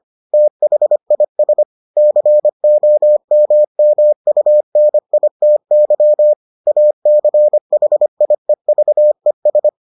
Morse code, heard as a keyed tone, about 25 wpm, 600 Hz. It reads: THIS COMMUNITY ACHIEVES